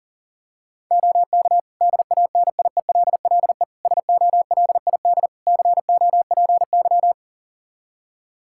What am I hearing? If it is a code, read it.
Morse code, heard as a keyed tone, about 40 wpm, 700 Hz. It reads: OK DANIELLE SOLID COPY
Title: Practice QSO - Exchange 3 - More Info - 40wpm